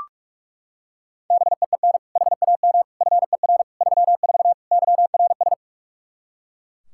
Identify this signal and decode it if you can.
Morse code, heard as a keyed tone, about 42 wpm, 700 Hz. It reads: BEEN HAM FER 34 YRS
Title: Practice QSO - Exchange 3 - More Info - 45wpm